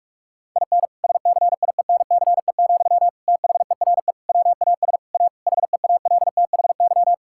Morse code, heard as a keyed tone, about 46 wpm, 700 Hz. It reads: IN SCIENCE, THERE WAS A HEALTHY
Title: Sentences from Top 1000 Words - Review Part 5 - 45wpm